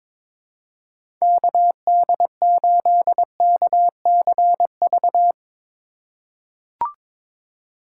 Morse code, heard as a keyed tone, about 22 wpm, 700 Hz. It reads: KD8KCV
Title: USA Call Signs - 6 Characters - 2x Character Spacing - 22wpm